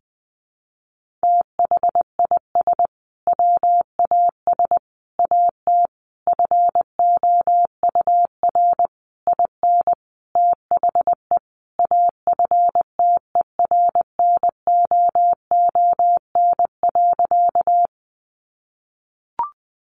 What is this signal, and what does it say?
Morse code, heard as a keyed tone, about 20 wpm, 700 Hz. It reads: THIS WAS AT FOUR IN THE AFTERNOON.
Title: Sentences from Top 1500 Words 20wpm